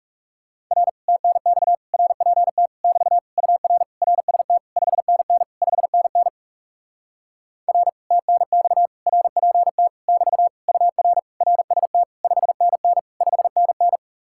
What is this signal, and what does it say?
Morse code, heard as a keyed tone, about 40 wpm, 700 Hz. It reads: R TNX RPT <BT> UR RST 5NN 5NN R TNX RPT <BT> UR RST 5NN 5NN